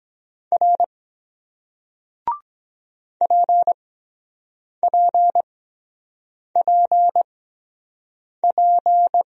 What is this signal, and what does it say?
Morse code, heard as a keyed tone, about 23 wpm, spaced as for 7 wpm, 700 Hz. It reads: R PPPP